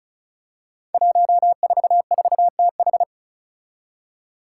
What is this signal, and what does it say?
Morse code, heard as a keyed tone, about 36 wpm, 700 Hz. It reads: 144TH
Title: Five Challenging Letter-Number Combinations 35wpm